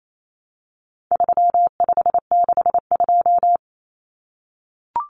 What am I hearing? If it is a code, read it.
Morse code, heard as a keyed tone, about 28 wpm, 700 Hz. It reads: 3562
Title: Numbers - 4-Digit 28wpm